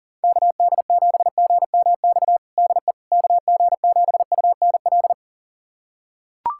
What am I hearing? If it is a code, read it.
Morse code, heard as a keyed tone, about 40 wpm, 690 Hz. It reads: KD7GMX DE KG7UNL
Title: USA Callsign DE Callsign - 6 Characters - 40wpm